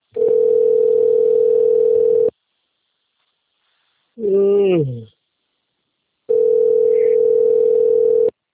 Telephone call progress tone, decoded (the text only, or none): ringback tone